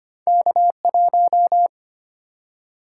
Morse code, heard as a keyed tone, about 25 wpm, 700 Hz. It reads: K1
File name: Two Letter-Number Combinations - 25wpm